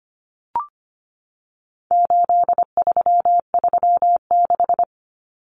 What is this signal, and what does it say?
Morse code, heard as a keyed tone, about 25 wpm, 700 Hz. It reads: 8336
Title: Common Cut Numbers - 4-Digit 25wpm